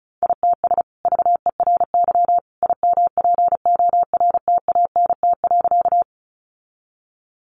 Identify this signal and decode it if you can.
Morse code, heard as a keyed tone, about 34 wpm, 700 Hz. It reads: ITS VERY IMPORTANT.
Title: Sentences from Top 600 Words 35wpm